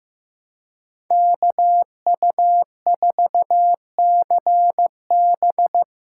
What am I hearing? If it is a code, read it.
Morse code, heard as a keyed tone, about 15 wpm, 700 Hz. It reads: KU4CB
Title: USA Call Signs - 3x Character Spacing - 15wpm